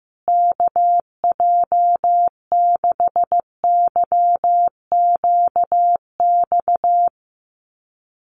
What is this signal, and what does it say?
Morse code, heard as a keyed tone, about 15 wpm, 700 Hz. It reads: KJ6YQX
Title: USA Call Signs Repeated 3x 15wpm